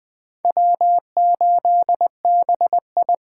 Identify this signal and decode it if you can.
Morse code, heard as a keyed tone, about 21 wpm, 700 Hz. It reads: W8BI
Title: Speed Racing - USA 1x2 Call Signs 17wpm